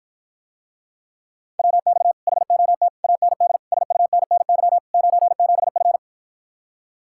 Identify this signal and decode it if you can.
Morse code, heard as a keyed tone, about 54 wpm, 700 Hz. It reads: WX HOT AND SUNNY 97F